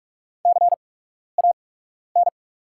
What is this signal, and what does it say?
Morse code, heard as a keyed tone, about 48 wpm, spaced as for 12 wpm, 690 Hz. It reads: CAN
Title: 8x Character Spacing - Top 300 Words - 45wpm